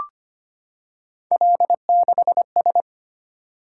Morse code, heard as a keyed tone, about 25 wpm, 700 Hz. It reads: L6S
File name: Three Letter-Number Combinations - 25wpm